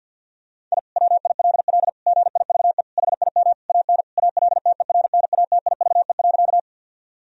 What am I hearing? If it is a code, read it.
Morse code, heard as a keyed tone, about 50 wpm, 700 Hz. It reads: I WILL GIVE HIM AN ALTERNATIVE.
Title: Sentences from Top 1000 Words - Review Part 4 - 50wpm